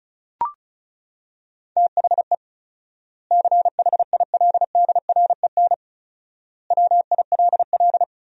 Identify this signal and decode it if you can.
Morse code, heard as a keyed tone, about 34 wpm, 700 Hz. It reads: THE CHILDREN WILL
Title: 4x Word-Spacing - Sentences from Top 300 Words - 35wpm